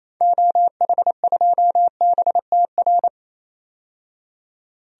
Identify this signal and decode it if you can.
Morse code, heard as a keyed tone, about 28 wpm, 700 Hz. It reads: OH2BTR
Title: DXCC-100 Call Signs - 6 Characters - 28wpm